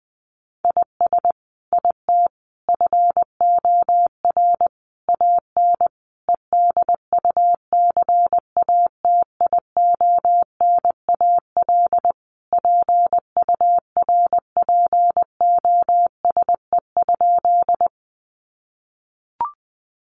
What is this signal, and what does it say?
Morse code, heard as a keyed tone, about 20 wpm, 700 Hz. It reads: IS IT FOR AN EDUCATIONAL PURPOSE?
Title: Sentences from Top 1200 Words 20wpm